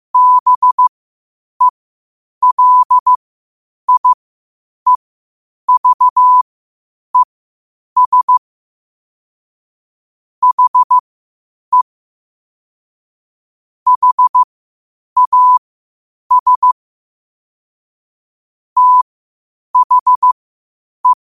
Morse code, heard as a keyed tone, about 15 wpm, spaced as for 8 wpm, 1000 Hz. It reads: BELIEVES HE HAS THE
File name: qotdfiles / 8wpm